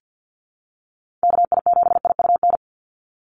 Morse code, heard as a keyed tone, about 50 wpm, 700 Hz. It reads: KI7IUN
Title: USA Call Signs 50wpm